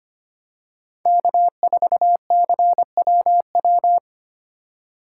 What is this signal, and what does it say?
Morse code, heard as a keyed tone, about 25 wpm, 700 Hz. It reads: K4CWW
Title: USA Call Signs - 5 Characters - 3x Character Spacing - 25wpm